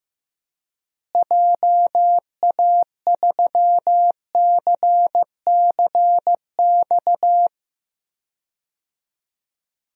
Morse code, heard as a keyed tone, about 15 wpm, 700 Hz. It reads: JA3CCX
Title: DXCC-100 Call Signs - 6 Characters - Repeated 3x - 15wpm